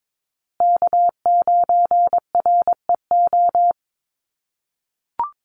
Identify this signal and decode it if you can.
Morse code, heard as a keyed tone, about 22 wpm, 700 Hz. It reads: K9REO
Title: USA Call Signs Repeated 3x 22wpm